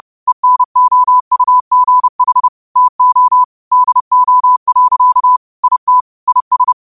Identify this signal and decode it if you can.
Morse code, heard as a keyed tone, about 30 wpm, 1000 Hz. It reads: ENOUGH TO DO. IT IS
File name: qotdfiles / 30wpm